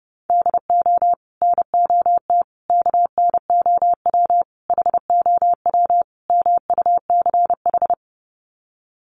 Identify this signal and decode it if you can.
Morse code, heard as a keyed tone, about 30 wpm, 700 Hz. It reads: DO NOT KNOW HOW MUCH
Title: Sets of 5 Words Encore 30wpm